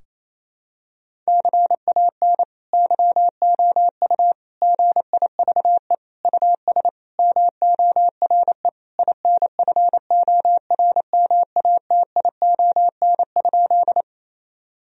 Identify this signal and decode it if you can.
Morse code, heard as a keyed tone, about 27 wpm, 700 Hz. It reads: CAN YOU GIVE US MORE INFORMATION?
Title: Sentences from Top 1000 Words - Review Part 3 - 28wpm